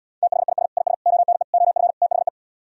Morse code, heard as a keyed tone, about 77 wpm, 700 Hz. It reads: THIS IS ONE OF THE